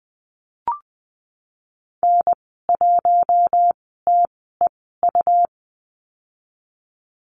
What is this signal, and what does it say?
Morse code, heard as a keyed tone, about 20 wpm, spaced as for 14 wpm, 700 Hz. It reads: N1TEU
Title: 154 Lesson 25g - Letter p - Call Signs 20wpm